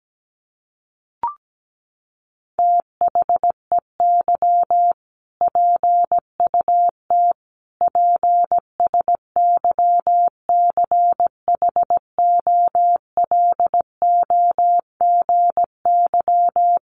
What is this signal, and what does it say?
Morse code, heard as a keyed tone, about 17 wpm, 700 Hz. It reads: THEY PUT PSYCHOLOGY
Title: Sentences from Top 1400 Words 17wpm